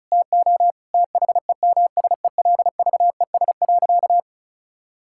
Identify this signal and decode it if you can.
Morse code, heard as a keyed tone, about 36 wpm, 700 Hz. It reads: TO THEMSELVES.
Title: Sentences from Top 1000 Words - Review Part 6 - 35wpm